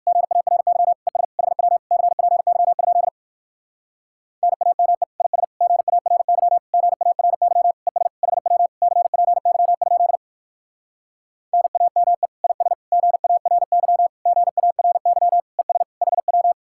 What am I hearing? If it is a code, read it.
Morse code, heard as a keyed tone, about 59 wpm, 700 Hz. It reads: GARY ES HW CPY? NAME IS GARY GARY ES HW CPY? NAME IS GARY GARY ES HW